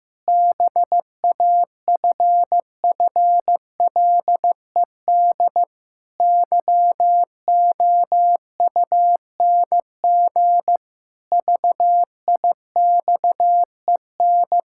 Morse code, heard as a keyed tone, about 15 wpm, 700 Hz. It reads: BAFFLED YOUNG VIXEN
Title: Pangram Challenge v2 - 15wpm